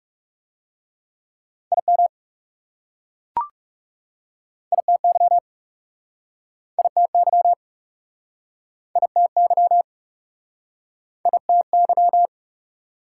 Morse code, heard as a keyed tone, about 40 wpm, 700 Hz. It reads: IM ITY ITY ITY ITY